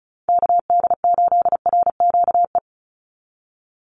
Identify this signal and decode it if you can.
Morse code, heard as a keyed tone, about 34 wpm, 700 Hz. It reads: KD8RQE
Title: Call Signs Repeated 2x - 35wpm